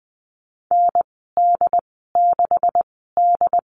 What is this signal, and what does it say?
Morse code, heard as a keyed tone, about 20 wpm, spaced as for 15 wpm, 700 Hz. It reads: ND6D
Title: 161 Lesson 26g - Letter g - Call Signs 20wpm